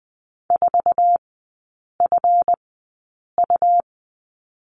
Morse code, heard as a keyed tone, about 20 wpm, spaced as for 8 wpm, 700 Hz. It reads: 4FU
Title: 138 Lesson 23e - Letter f - Three Characters 20wpm